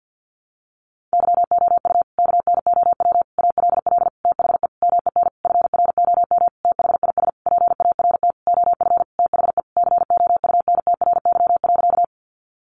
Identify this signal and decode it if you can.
Morse code, heard as a keyed tone, about 50 wpm, 700 Hz. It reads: YOU KNOW ALL THE MEN FROM THIS PART OF THE COUNTRY.
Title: Sentences from Top 300 Words 50wpm